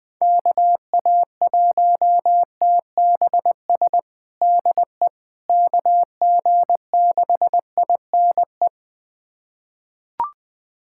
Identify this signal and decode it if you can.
Morse code, heard as a keyed tone, about 20 wpm, 700 Hz. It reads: KA1TBS DE KG6INE
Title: USA Callsign DE Callsign 20wpm